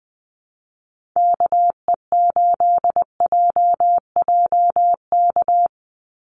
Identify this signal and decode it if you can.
Morse code, heard as a keyed tone, about 20 wpm, 700 Hz. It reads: KE8JJK
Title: USA Call Signs 20wpm